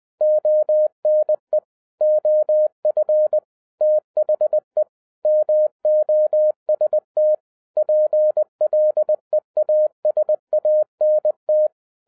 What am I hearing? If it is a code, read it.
Morse code, heard as a keyed tone, about 20 wpm, 610 Hz. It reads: ONE OF THE MOST PLEASANT
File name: 2011-07-mp3 / n43 Czech Army Morse Keys 20wpm